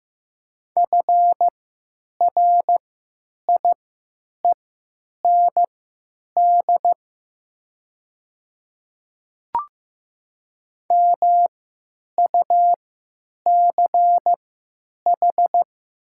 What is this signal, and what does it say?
Morse code, heard as a keyed tone, about 15 wpm, spaced as for 8 wpm, 700 Hz. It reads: FRIEND MUCH
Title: 3x Character Spacing - Top 500 Words - 15wpm